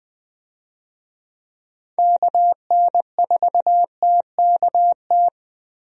Morse code, heard as a keyed tone, about 20 wpm, 700 Hz. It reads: KN4TKT
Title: USA Call Signs - 6 Characters - Repeated 3x - 20wpm